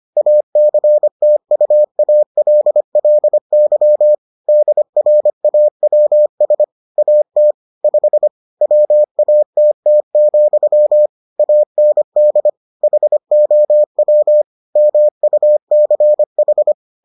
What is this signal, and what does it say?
Morse code, heard as a keyed tone, about 25 wpm, 600 Hz. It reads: ACTUALLY DRAWS AT 5 WATT, AND HOW MUCH